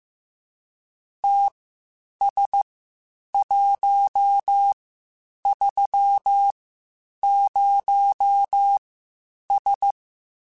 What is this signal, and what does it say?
Morse code, heard as a keyed tone, about 15 wpm, spaced as for 8 wpm, 800 Hz. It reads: TS130S
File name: QSOs / QSOs-mp3s / WD0BGP 1@10